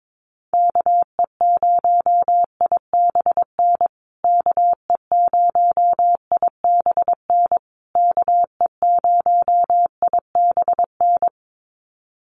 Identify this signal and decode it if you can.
Morse code, heard as a keyed tone, about 22 wpm, 700 Hz. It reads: KE0IBN KE0IBN KE0IBN
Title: USA Call Signs Repeated 3x 22wpm